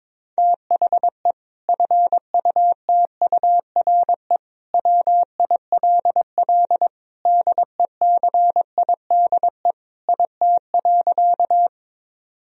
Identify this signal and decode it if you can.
Morse code, heard as a keyed tone, about 22 wpm, 700 Hz. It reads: THE FUTURE WILL DECIDE IT.
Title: Sentences from Top 1000 Words - Review Part 3 - 22wpm